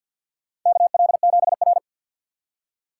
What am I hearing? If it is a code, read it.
Morse code, heard as a keyed tone, about 48 wpm, 700 Hz. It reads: KL7R
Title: USA Call Signs - 4 Characters - 6x Character Spacing - 50wpm